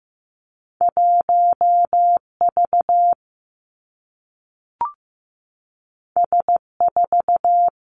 Morse code, heard as a keyed tone, about 15 wpm, 700 Hz. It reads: 1 V S 4
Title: Two Challenging Letter-Number Combinations 15wpm